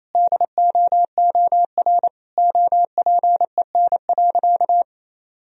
Morse code, heard as a keyed tone, about 28 wpm, 700 Hz. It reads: DOOR OPEN.
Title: Sentences from Top 1000 Words - Review Part 3 - 28wpm